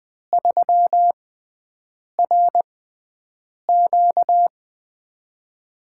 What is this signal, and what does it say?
Morse code, heard as a keyed tone, about 20 wpm, spaced as for 7 wpm, 700 Hz. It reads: 3RQ
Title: USA Call Signs - 5 Characters - 6x Character Spacing - 20wpm